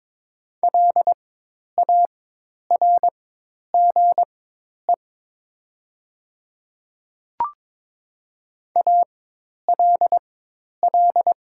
Morse code, heard as a keyed tone, about 22 wpm, spaced as for 10 wpm, 700 Hz. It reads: LARGE ALL
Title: 4x Character Spacing - Top 200 Words - 22wpm